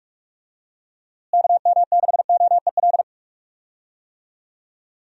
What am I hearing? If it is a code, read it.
Morse code, heard as a keyed tone, about 42 wpm, 700 Hz. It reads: KM6OEL
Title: USA Call Signs - 6 Characters - Repeated 3x - 45wpm